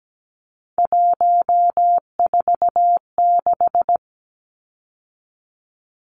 Morse code, heard as a keyed tone, about 17 wpm, 700 Hz. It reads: 146
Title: Numbers - 3-Digit 17wpm